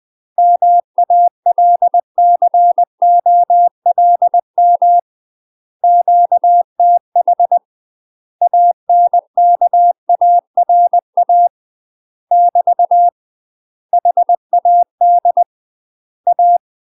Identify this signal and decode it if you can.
Morse code, heard as a keyed tone, about 20 wpm, 700 Hz. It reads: MALCOLM QTH ANKARA <BT> HAD A
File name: UK MASTER QSOs TRAINING / 20WPM / UK MASTER QSOs 0011